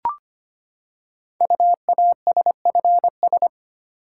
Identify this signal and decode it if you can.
Morse code, heard as a keyed tone, about 25 wpm, 690 Hz. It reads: UASFS
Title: Code Groups - 5 Letters - 25wpm